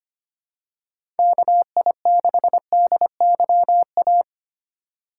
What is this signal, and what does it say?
Morse code, heard as a keyed tone, about 25 wpm, 700 Hz. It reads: KI6DYA
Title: USA Call Signs - 2x Character Spacing - 25wpm